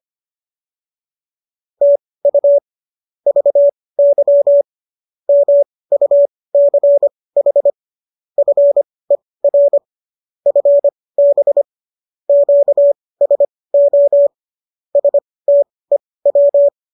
Morse code, heard as a keyed tone, about 25 wpm, spaced as for 18 wpm, 570 Hz. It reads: TU VY MUCH FER FB QSO STEW
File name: qso218 18